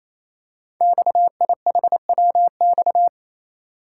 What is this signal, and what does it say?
Morse code, heard as a keyed tone, about 28 wpm, 700 Hz. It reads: XIHWX
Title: Code Groups - 5 Letters - 28wpm